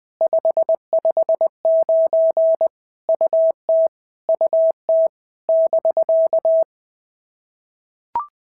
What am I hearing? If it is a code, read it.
Morse code, heard as a keyed tone, about 20 wpm, 650 Hz. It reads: 559 UT UT <BK>
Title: POTA QSO Practice - 20wpm